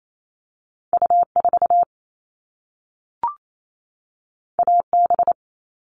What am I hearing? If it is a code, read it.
Morse code, heard as a keyed tone, about 28 wpm, 700 Hz. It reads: U4 AB
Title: All Cut Numbers - 2-Digit 28wpm